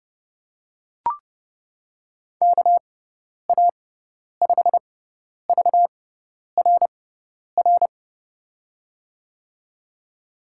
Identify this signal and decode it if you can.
Morse code, heard as a keyed tone, about 30 wpm, spaced as for 10 wpm, 700 Hz. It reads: KA5VRR
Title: USA Call Signs - 6 Characters - 6x Character Spacing - 30wpm